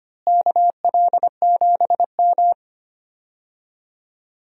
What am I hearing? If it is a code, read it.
Morse code, heard as a keyed tone, about 25 wpm, 700 Hz. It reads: KL7M
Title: DXCC-100 Call Signs - 4 Characters - Repeated 3x - 25wpm